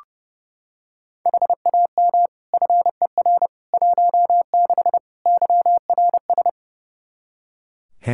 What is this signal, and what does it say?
Morse code, heard as a keyed tone, about 32 wpm, 700 Hz. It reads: HAM FER 16 YRS
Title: Practice QSO - Exchange 3 - More Info - 30wpm